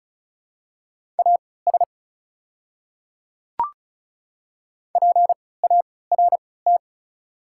Morse code, heard as a keyed tone, about 36 wpm, spaced as for 20 wpm, 690 Hz. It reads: AS PART
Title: 3x Character Spacing - Top 300 Words - 35wpm